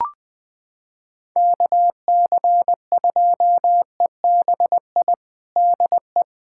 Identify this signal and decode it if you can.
Morse code, heard as a keyed tone, about 20 wpm, 700 Hz. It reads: KC2EBI DE
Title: USA Callsign DE Callsign 20wpm